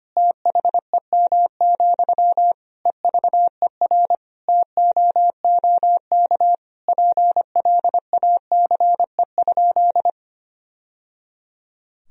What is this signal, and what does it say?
Morse code, heard as a keyed tone, about 25 wpm, 700 Hz. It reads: THEM, EVER TOOK PLACE?
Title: Sentences from Top 1000 Words - Review Part 1 - 25wpm